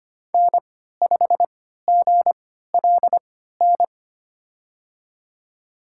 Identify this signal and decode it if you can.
Morse code, heard as a keyed tone, about 25 wpm, spaced as for 14 wpm, 700 Hz. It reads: N5GLN
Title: USA Call Signs - 5 Characters - 3x Character Spacing - 25wpm